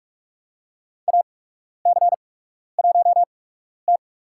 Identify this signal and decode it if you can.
Morse code, heard as a keyed tone, about 46 wpm, spaced as for 12 wpm, 700 Hz. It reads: AC1T
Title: USA Call Signs - 8x Character Spacing - 45wpm